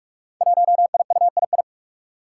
Morse code, heard as a keyed tone, about 44 wpm, 700 Hz. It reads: 1IUII
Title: Five Challenging Letter-Number Combinations 45wpm